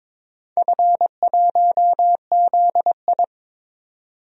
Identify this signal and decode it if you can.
Morse code, heard as a keyed tone, about 22 wpm, 700 Hz. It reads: F1ZI
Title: DXCC-100 Call Signs - 4 Characters - 22wpm